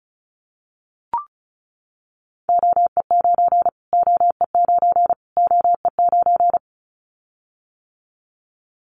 Morse code, heard as a keyed tone, about 34 wpm, 700 Hz. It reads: OE9 OE9 OE9
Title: Callsign Prefixes for Top 25 DXCC - Repeated 3x - 35wpm